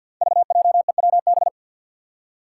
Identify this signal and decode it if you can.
Morse code, heard as a keyed tone, about 52 wpm, 700 Hz. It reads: VJEWB
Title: Five Challenging Letter-Number Combinations 50wpm